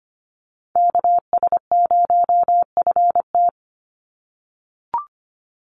Morse code, heard as a keyed tone, about 25 wpm, 700 Hz. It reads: KS0FT
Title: USA Call Signs 25wpm